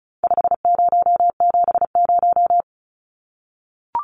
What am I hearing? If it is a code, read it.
Morse code, heard as a keyed tone, about 34 wpm, 700 Hz. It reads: 5070
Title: Numbers - 4-Digit 35wpm